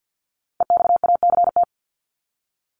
Morse code, heard as a keyed tone, about 51 wpm, 700 Hz. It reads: EXACT
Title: Top 1251-1500 Words 50wpm